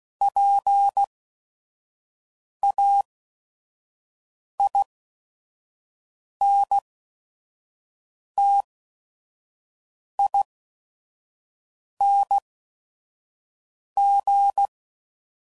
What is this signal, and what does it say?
Morse code, heard as a keyed tone, about 16 wpm, spaced as for 5 wpm, 800 Hz. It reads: PAINTING
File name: Suffixes / Special ING's / VerySlowToFaster ING's